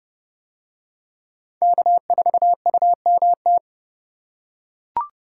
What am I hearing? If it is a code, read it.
Morse code, heard as a keyed tone, about 28 wpm, 700 Hz. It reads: K4UMT